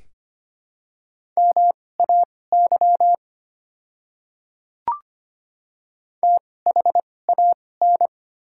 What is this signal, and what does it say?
Morse code, heard as a keyed tone, about 25 wpm, spaced as for 18 wpm, 700 Hz. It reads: MAY THAN
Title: 2x Character Spacing - Top 100 Words - 25wpm